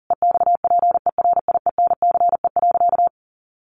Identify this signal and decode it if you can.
Morse code, heard as a keyed tone, about 37 wpm, 700 Hz. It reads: EXPERIENCE.
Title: Sentences from Top 600 Words 40wpm